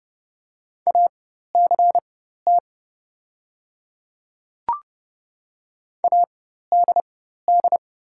Morse code, heard as a keyed tone, about 30 wpm, spaced as for 14 wpm, 700 Hz. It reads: ACT ADD
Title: 4x Character Spacing - Top 300 Words - 30wpm